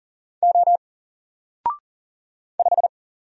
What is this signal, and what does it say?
Morse code, heard as a keyed tone, about 40 wpm, 700 Hz. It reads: O 5
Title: Single Character (Letter:Number:Punctuation:Prosign) 40wpm